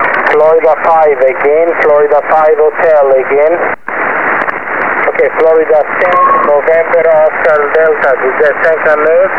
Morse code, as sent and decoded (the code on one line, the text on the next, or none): none